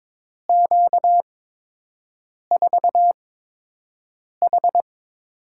Morse code, heard as a keyed tone, about 22 wpm, spaced as for 6 wpm, 700 Hz. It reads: Q4H
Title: USA Call Signs - 8x Character Spacing - 22wpm